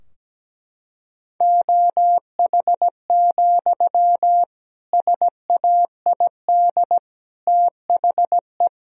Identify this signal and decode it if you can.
Morse code, heard as a keyed tone, about 17 wpm, 700 Hz. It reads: OH, SAID THE
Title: Sentences from Top 1000 Words - Review Part 2 - 17wpm